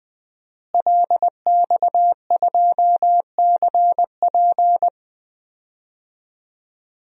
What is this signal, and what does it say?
Morse code, heard as a keyed tone, about 20 wpm, 700 Hz. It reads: LX2CP
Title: DXCC-100 Call Signs - 5 Characters - Repeated 2x - 20wpm